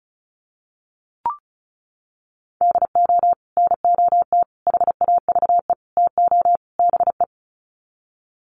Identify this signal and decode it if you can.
Morse code, heard as a keyed tone, about 34 wpm, 700 Hz. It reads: DO NOT HAVE TO BE
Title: Sets of 5 Words 35wpm